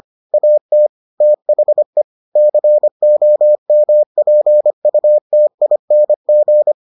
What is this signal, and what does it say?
Morse code, heard as a keyed tone, about 25 wpm, 600 Hz. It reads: AT THE COMPUTING